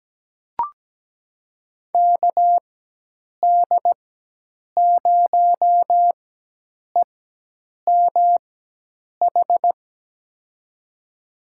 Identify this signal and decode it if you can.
Morse code, heard as a keyed tone, about 17 wpm, spaced as for 8 wpm, 700 Hz. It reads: KD0EMH
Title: USA Call Signs - 4x Character Spacing - 17wpm